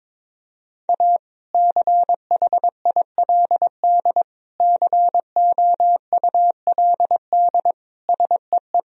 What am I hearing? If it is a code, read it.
Morse code, heard as a keyed tone, about 21 wpm, 700 Hz. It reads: A CHILD COULD SEE
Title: Sentences from Top 1000 Words - Review Part 1 - 22wpm